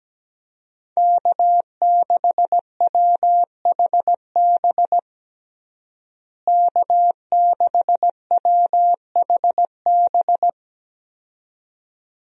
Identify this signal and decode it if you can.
Morse code, heard as a keyed tone, about 17 wpm, 700 Hz. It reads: K6WHB K6WHB
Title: USA Call Signs - 5 Characters - Repeated 2x - 17wpm